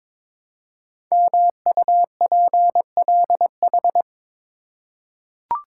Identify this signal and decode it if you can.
Morse code, heard as a keyed tone, about 23 wpm, 700 Hz. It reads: MUPLH
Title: Code Groups - 5 Letters - 22wpm